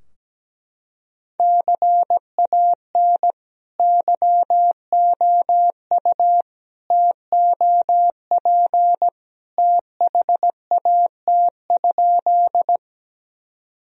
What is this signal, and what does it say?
Morse code, heard as a keyed tone, about 17 wpm, 700 Hz. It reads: CAN YOU TOP THAT?
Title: Sentences from Top 1000 Words - Review Part 2 - 17wpm